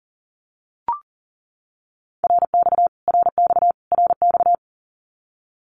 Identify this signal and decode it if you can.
Morse code, heard as a keyed tone, about 40 wpm, 700 Hz. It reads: RX RX RX
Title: Abbreviations Repeated 3x - 40wpm